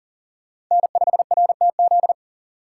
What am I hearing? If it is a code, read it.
Morse code, heard as a keyed tone, about 40 wpm, 700 Hz. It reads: N5RTZ
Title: USA Call Signs - 4x Character Spacing - 40wpm